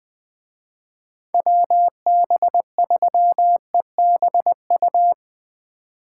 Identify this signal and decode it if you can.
Morse code, heard as a keyed tone, about 20 wpm, 700 Hz. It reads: WB3EBU